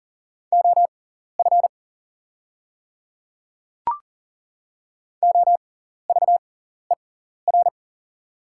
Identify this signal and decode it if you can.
Morse code, heard as a keyed tone, about 40 wpm, spaced as for 14 wpm, 700 Hz. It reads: OF OVER
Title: 6x Character Spacing - Top 100 Words - 40wpm